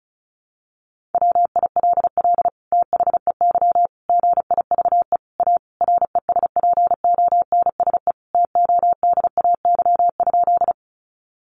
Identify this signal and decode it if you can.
Morse code, heard as a keyed tone, about 34 wpm, 700 Hz. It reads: WILL THEY GIVE A RESPONSE TODAY?
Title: Sentences from Top 1400 Words 35wpm